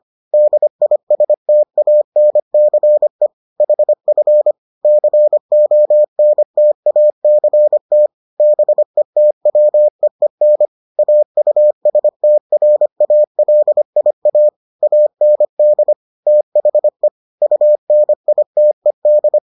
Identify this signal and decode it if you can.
Morse code, heard as a keyed tone, about 25 wpm, 600 Hz. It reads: DISTANCE HF CONTACT BETWEEN AUSTRALIA AND THE UNITED